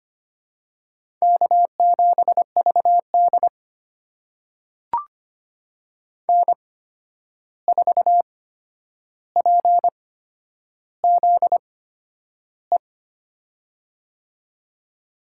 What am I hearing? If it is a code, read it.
Morse code, heard as a keyed tone, about 25 wpm, spaced as for 10 wpm, 700 Hz. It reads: K7VD N4PZE
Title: USA Call Signs - 8x Character Spacing - 25wpm